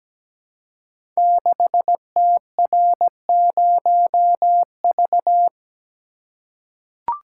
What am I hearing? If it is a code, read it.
Morse code, heard as a keyed tone, about 17 wpm, 700 Hz. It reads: 6TR0V
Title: Code Groups - 5 Letter:Numbers - 17wpm